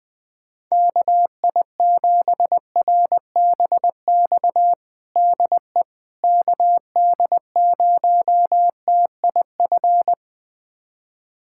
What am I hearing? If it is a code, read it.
Morse code, heard as a keyed tone, about 21 wpm, 700 Hz. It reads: KI7RBX DE KD0TIF
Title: USA Callsign DE Callsign - 6 Characters - 20wpm